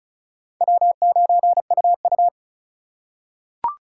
Morse code, heard as a keyed tone, about 34 wpm, 700 Hz. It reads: W9UU
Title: USA Call Signs - 4 Characters - 6x Character Spacing - 35wpm